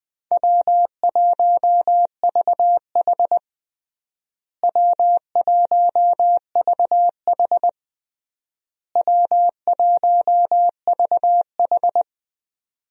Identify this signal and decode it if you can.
Morse code, heard as a keyed tone, about 21 wpm, 700 Hz. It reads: W1VH W1VH W1VH
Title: USA Call Signs - 4 Characters - Repeated 3x - 20wpm